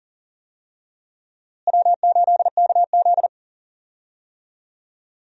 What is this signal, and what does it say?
Morse code, heard as a keyed tone, about 37 wpm, 700 Hz. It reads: W8KZ
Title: USA Call Signs - 4 Characters - Repeated 2x - 40wpm